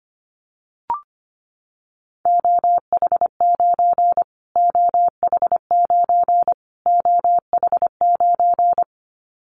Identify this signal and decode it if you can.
Morse code, heard as a keyed tone, about 25 wpm, 700 Hz. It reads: OH9 OH9 OH9
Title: Callsign Prefixes for Top 25 DXCC - Repeated 3x - 25wpm